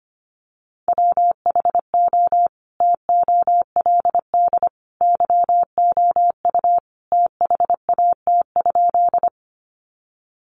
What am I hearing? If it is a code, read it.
Morse code, heard as a keyed tone, about 25 wpm, 700 Hz. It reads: WHO TOLD YOU THAT?
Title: Sentences from Top 400 Words 25wpm